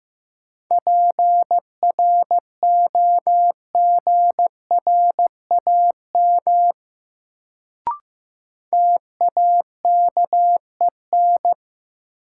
Morse code, heard as a keyed tone, about 15 wpm, 700 Hz. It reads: PROGRAM TAKEN
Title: Sentences from Top 1000 Words - Review Part 4 - New Word Drill - 15wpm